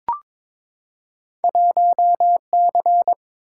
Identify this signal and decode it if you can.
Morse code, heard as a keyed tone, about 22 wpm, 700 Hz. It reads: 1C
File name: practice / 22wpm / Two Letter-Number Combinations - 22wpm